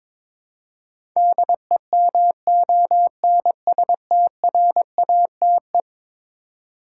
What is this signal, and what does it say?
Morse code, heard as a keyed tone, about 22 wpm, 700 Hz. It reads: DEMONSTRATE
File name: practice / 22wpm / Eleven Letter Words - 22wpm